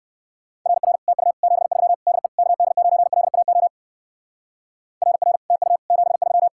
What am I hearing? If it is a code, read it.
Morse code, heard as a keyed tone, about 59 wpm, 700 Hz. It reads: RR TU 73 DE KD9LAO RR TU 73